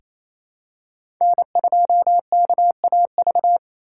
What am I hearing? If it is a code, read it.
Morse code, heard as a keyed tone, about 28 wpm, 700 Hz. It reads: N2KAV
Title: USA Call Signs - 5 Characters - 8x Character Spacing - 28wpm